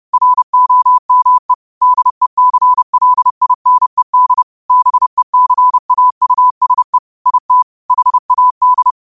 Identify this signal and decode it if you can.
Morse code, heard as a keyed tone, about 30 wpm, 1000 Hz. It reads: ROME DECLINED BECAUSE IT HAD
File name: qotdfiles / 30wpm